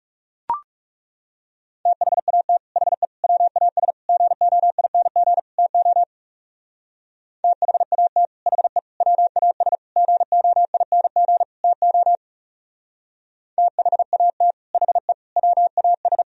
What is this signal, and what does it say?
Morse code, heard as a keyed tone, about 37 wpm, 700 Hz. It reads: THAT HE WAS GOING TO THAT HE WAS GOING TO THAT HE WAS
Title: Speed-Racing - Sets of 5 Words 30wpm